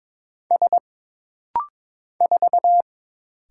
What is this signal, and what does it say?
Morse code, heard as a keyed tone, about 22 wpm, 700 Hz. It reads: S 4
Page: This rapid-fire timing is geared to push your ICR (Instant Character Recognition) skills.